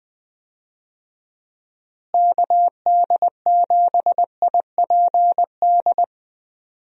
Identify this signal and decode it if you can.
Morse code, heard as a keyed tone, about 20 wpm, 700 Hz. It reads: KD7IPD
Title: USA Call Signs - 6 Characters - Repeated 2x - 20wpm